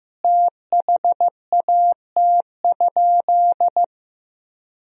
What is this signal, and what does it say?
Morse code, heard as a keyed tone, about 15 wpm, 700 Hz. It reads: THAT?
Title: Sentences from Top 1000 Words - Review Part 2 - 15wpm